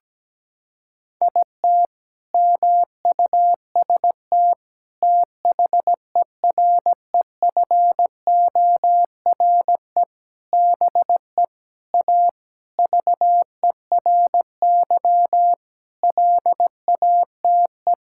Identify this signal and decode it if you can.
Morse code, heard as a keyed tone, about 17 wpm, 700 Hz. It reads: IT MUST THEREFORE BE A VERY LATE
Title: Sentences from Top 1000 Words - Review Part 6 - 17wpm